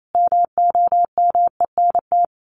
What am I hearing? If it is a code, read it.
Morse code, heard as a keyed tone, about 28 wpm, 700 Hz. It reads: MOMENT
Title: Sets of 5 Words Encore 28wpm